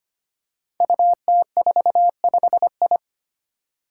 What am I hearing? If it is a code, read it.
Morse code, heard as a keyed tone, about 25 wpm, 700 Hz. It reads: UT45I
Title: Five Challenging Letter-Number Combinations 25wpm